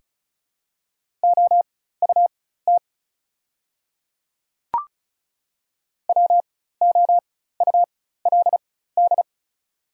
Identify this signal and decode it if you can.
Morse code, heard as a keyed tone, about 36 wpm, spaced as for 16 wpm, 700 Hz. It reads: OUT WOULD
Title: 4x Character Spacing - Top 100 Words - 35wpm